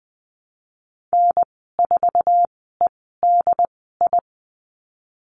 Morse code, heard as a keyed tone, about 20 wpm, spaced as for 15 wpm, 700 Hz. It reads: N4EDI
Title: 069 Lesson 12g - Letter d - Call Signs 20wpm